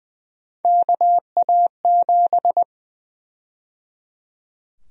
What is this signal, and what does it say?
Morse code, heard as a keyed tone, about 21 wpm, 700 Hz. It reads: KA7
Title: USA Call Sign Prefixes - 20wpm